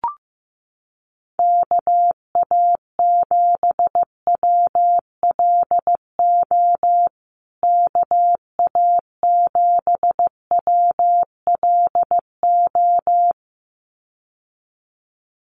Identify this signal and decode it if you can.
Morse code, heard as a keyed tone, about 15 wpm, 700 Hz. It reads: KA7WLO KA7WLO
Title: USA Call Signs Repeated 2x 15wpm